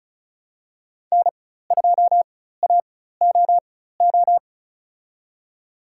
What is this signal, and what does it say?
Morse code, heard as a keyed tone, about 34 wpm, spaced as for 16 wpm, 700 Hz. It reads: N2AOO
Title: USA Call Signs - 5 Characters - 4x Character Spacing - 35wpm